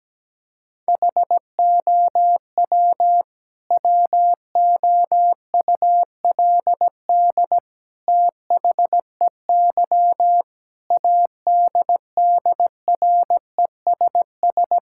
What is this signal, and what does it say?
Morse code, heard as a keyed tone, about 17 wpm, 700 Hz. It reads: HOW WOULD THEY ADDRESS
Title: Sentences from Top 1000 Words - Review Part 6 - 17wpm